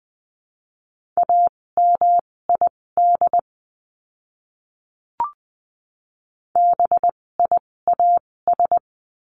Morse code, heard as a keyed tone, about 20 wpm, spaced as for 16 wpm, 700 Hz. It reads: AMID BIAS
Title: 256 Lesson 42a - Words 12-20wpm